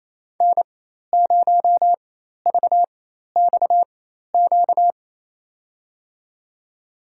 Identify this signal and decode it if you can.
Morse code, heard as a keyed tone, about 28 wpm, spaced as for 13 wpm, 700 Hz. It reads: N0VXQ
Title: USA Call Signs - 5 Characters - 4x Character Spacing - 28wpm